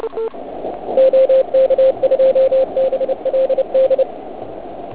Morse code, heard as a keyed tone, about 32 wpm, 530 Hz. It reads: OK2BLD
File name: Archiv / CUC106 / cuc106 PILEUP